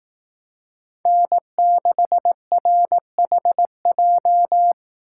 Morse code, heard as a keyed tone, about 18 wpm, 700 Hz. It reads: N6RHJ
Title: Speed Racing - USA Call Signs 15wpm